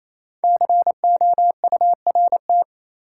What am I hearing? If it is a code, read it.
Morse code, heard as a keyed tone, about 28 wpm, 700 Hz. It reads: COURT
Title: Sentences from Top 1600 Words - New Word Drill - 28wpm